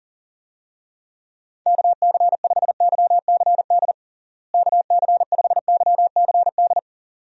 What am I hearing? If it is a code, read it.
Morse code, heard as a keyed tone, about 37 wpm, 700 Hz. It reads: KC5YCD KC5YCD
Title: USA Call Signs - 6 Characters - Repeated 2x - 40wpm